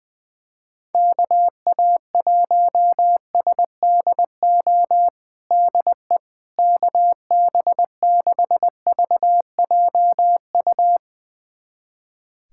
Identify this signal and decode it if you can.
Morse code, heard as a keyed tone, about 20 wpm, 700 Hz. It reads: KA1SDO DE KB6VJU
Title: USA Callsign DE Callsign - 6 Characters - 20wpm